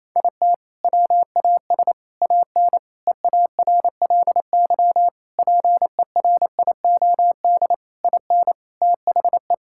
Morse code, heard as a keyed tone, about 28 wpm, 700 Hz. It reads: IT WAS AN EARLY PERIOD IN THE
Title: Sentences from Top 1000 Words - Review Part 5 - 28wpm